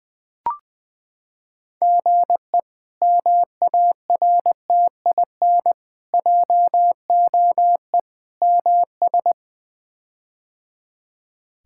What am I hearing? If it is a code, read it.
Morse code, heard as a keyed tone, about 21 wpm, 700 Hz. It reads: GE MARTIN JOE MS
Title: K1USN SST QSO Practice - 20wpm